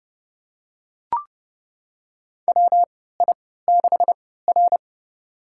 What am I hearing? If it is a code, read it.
Morse code, heard as a keyed tone, about 30 wpm, spaced as for 17 wpm, 700 Hz. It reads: WI6R
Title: USA Call Signs - 4 Characters - 3x Character Spacing - 30wpm